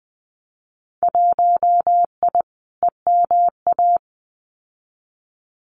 Morse code, heard as a keyed tone, about 20 wpm, 700 Hz. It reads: 1I EMA
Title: Winter Field Day Exchanges Encore 20wpm